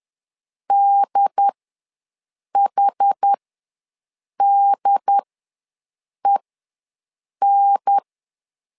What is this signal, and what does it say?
Morse code, heard as a keyed tone, about 10 wpm, spaced as for 6 wpm, 800 Hz. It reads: DHDEN
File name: LEZIONI / 2 UHDN / 2 UHDN ws 10wpm 2044